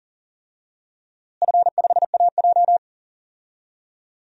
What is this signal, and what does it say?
Morse code, heard as a keyed tone, about 40 wpm, 700 Hz. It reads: F5AJ
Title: DXCC-100 Call Signs - 4 Characters - Repeated 3x - 40wpm